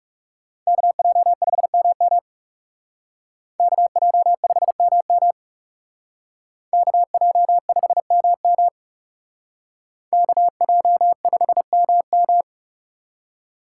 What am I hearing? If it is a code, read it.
Morse code, heard as a keyed tone, about 35 wpm, 700 Hz. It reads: KJ5MM KJ5MM KJ5MM KJ5MM
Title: Speed Racing - USA 2x2 Call Signs 30wpm